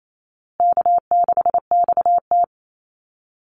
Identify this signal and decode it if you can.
Morse code, heard as a keyed tone, about 28 wpm, 700 Hz. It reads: K6XT
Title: Easy Call Signs Repeated 3x - 28wpm